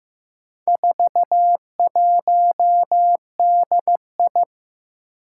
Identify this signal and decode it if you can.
Morse code, heard as a keyed tone, about 16 wpm, 700 Hz. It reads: 41DI
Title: Four Challenging Letter-Number Combinations 15wpm